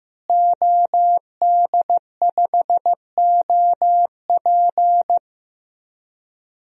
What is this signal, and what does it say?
Morse code, heard as a keyed tone, about 15 wpm, 700 Hz. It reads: OD5OP
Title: DXCC-100 Call Signs - 5 Characters - Repeated 2x - 15wpm